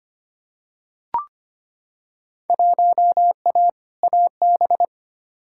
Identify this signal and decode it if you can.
Morse code, heard as a keyed tone, about 25 wpm, 700 Hz. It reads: 1A AB
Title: Field Day Exchanges - 25wpm